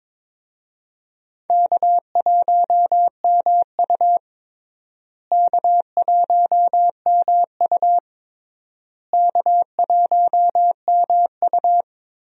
Morse code, heard as a keyed tone, about 22 wpm, 700 Hz. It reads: K1MU K1MU K1MU
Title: USA Call Signs - 4 Characters - Repeated 3x - 22wpm